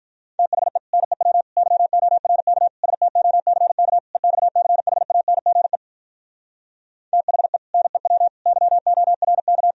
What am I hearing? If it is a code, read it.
Morse code, heard as a keyed tone, about 52 wpm, 700 Hz. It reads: THE NEW YORK STOCK EXCHANGE THE NEW YORK